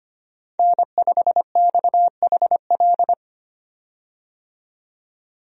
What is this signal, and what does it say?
Morse code, heard as a keyed tone, about 25 wpm, 700 Hz. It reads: N5XHL
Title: USA Call Signs - 5 Characters - 25wpm